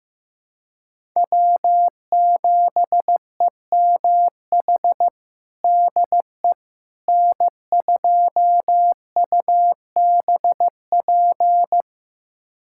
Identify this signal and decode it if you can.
Morse code, heard as a keyed tone, about 15 wpm, 700 Hz. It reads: W7EMH DE N2UBP
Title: USA Callsign DE Callsign - 5 Characters - 15wpm